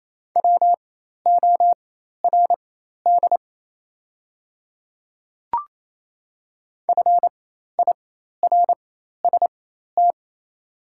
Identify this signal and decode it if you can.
Morse code, heard as a keyed tone, about 28 wpm, spaced as for 13 wpm, 690 Hz. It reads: WORD FIRST
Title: 4x Character Spacing - Top 500 Words - 28wpm